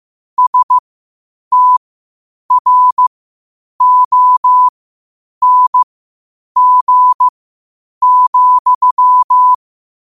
Morse code, heard as a keyed tone, about 15 wpm, spaced as for 8 wpm, 1000 Hz. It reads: STRONG,
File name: qotdfiles / 8wpm